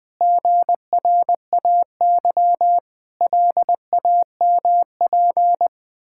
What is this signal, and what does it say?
Morse code, heard as a keyed tone, about 21 wpm, 700 Hz. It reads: GRAY LAMP
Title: Pangram Challenge v2 - 20wpm